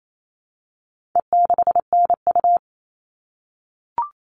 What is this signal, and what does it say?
Morse code, heard as a keyed tone, about 28 wpm, 700 Hz. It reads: E6NU
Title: All Cut Numbers - 4-Digit 28wpm